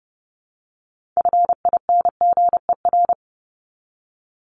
Morse code, heard as a keyed tone, about 30 wpm, 700 Hz. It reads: FINGER
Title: Top 1501-1750 Words 30wpm